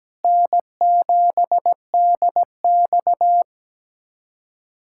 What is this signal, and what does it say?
Morse code, heard as a keyed tone, about 17 wpm, 700 Hz. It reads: N7DX
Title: USA Callsign DE Callsign - 4 Characters - 17wpm